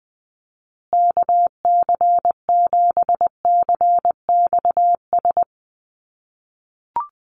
Morse code, heard as a keyed tone, about 20 wpm, 700 Hz. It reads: KC7CXS
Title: USA Call Signs Repeated 3x 20wpm